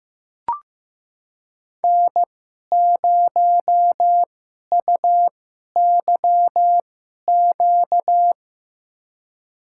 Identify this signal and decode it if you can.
Morse code, heard as a keyed tone, about 15 wpm, spaced as for 11 wpm, 700 Hz. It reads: N0UYQ
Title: USA Call Signs - 2x Character Spacing - 15wpm